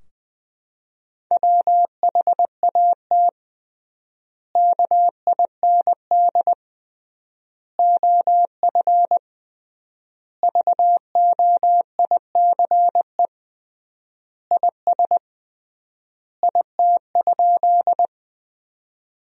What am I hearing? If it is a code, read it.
Morse code, heard as a keyed tone, about 20 wpm, 700 Hz. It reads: WHAT KIND OF VOICE IS IT?
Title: 3x Word-Spacing - Sentences from Top 500 Words - 20wpm